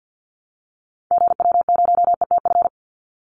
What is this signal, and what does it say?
Morse code, heard as a keyed tone, about 50 wpm, 700 Hz. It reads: GW0ETF
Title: Call Signs Repeated 3x - 50wpm